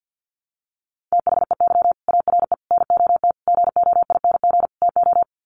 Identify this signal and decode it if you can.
Morse code, heard as a keyed tone, about 50 wpm, 700 Hz. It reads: THEY ARE NOT GOING TO
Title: Sets of 5 Words Encore2 50wpm